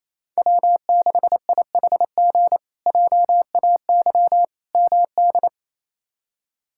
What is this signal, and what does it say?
Morse code, heard as a keyed tone, about 28 wpm, 700 Hz. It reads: W6IHG JAY MD
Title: K1USN SST QSO Practice - 28wpm